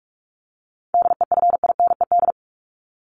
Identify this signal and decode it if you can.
Morse code, heard as a keyed tone, about 44 wpm, 700 Hz. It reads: DEFINED
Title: ED Suffix - Top 100 Words with Base Words 45wpm